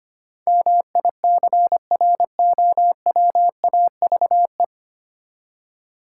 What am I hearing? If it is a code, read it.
Morse code, heard as a keyed tone, about 25 wpm, 700 Hz. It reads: MICROWAVE
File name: practice / 25wpm / Easy Words with Hints - 25wpm